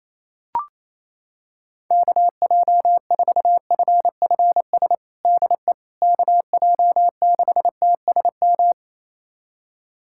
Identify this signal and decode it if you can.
Morse code, heard as a keyed tone, about 28 wpm, 700 Hz. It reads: KJ4FFS DE KJ6TSM
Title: USA Callsign DE Callsign - 6 Characters - 28wpm